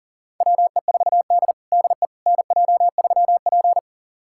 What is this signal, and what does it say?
Morse code, heard as a keyed tone, about 40 wpm, 700 Hz. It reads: WE4D DE NJ3P
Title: USA Callsign DE Callsign - 4 Characters - 40wpm